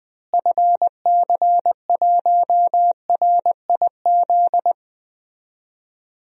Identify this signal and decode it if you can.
Morse code, heard as a keyed tone, about 20 wpm, 700 Hz. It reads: FC1RIZ
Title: DXCC-100 Call Signs - 6 Characters - Repeated 2x - 20wpm